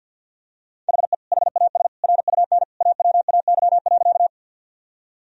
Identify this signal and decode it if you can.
Morse code, heard as a keyed tone, about 53 wpm, 700 Hz. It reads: HE HAS RUN AWAY.
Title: Sentences from Top 1000 Words - Review Part 1 - 50wpm